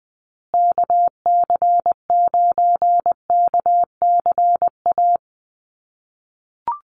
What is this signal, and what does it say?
Morse code, heard as a keyed tone, about 20 wpm, 700 Hz. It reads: KC9KCA
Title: USA Call Signs Repeated 3x 20wpm